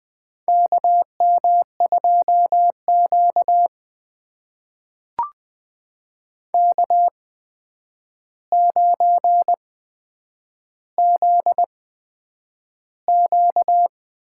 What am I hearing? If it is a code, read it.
Morse code, heard as a keyed tone, about 21 wpm, spaced as for 9 wpm, 700 Hz. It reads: KM2Q K9ZQ
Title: USA Call Signs - 4 Characters - 8x Character Spacing - 20wpm